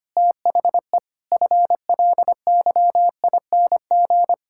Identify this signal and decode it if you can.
Morse code, heard as a keyed tone, about 25 wpm, 700 Hz. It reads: THE FLYING
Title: Pangram Challenge v2 - 25wpm